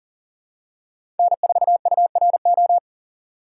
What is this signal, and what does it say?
Morse code, heard as a keyed tone, about 40 wpm, 700 Hz. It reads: N4URO
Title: USA Call Signs - 5 Characters - 3x Character Spacing - 40wpm